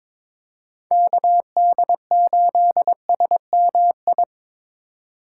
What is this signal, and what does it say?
Morse code, heard as a keyed tone, about 21 wpm, 700 Hz. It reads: KD8SMI
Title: USA Call Signs - 3x Character Spacing - 22wpm